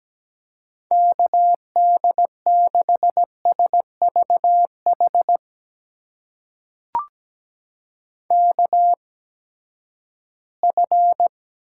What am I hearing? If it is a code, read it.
Morse code, heard as a keyed tone, about 17 wpm, spaced as for 12 wpm, 700 Hz. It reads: KD6SVH KF
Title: USA Call Signs - 6 Characters - 8x Character Spacing - 17wpm